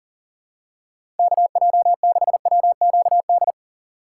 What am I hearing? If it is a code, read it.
Morse code, heard as a keyed tone, about 40 wpm, 700 Hz. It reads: KJ6WQD
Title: USA Call Signs - 3x Character Spacing - 40wpm